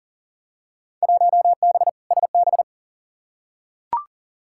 Morse code, heard as a keyed tone, about 40 wpm, 700 Hz. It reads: 1B SB
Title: Field Day Exchanges - 40wpm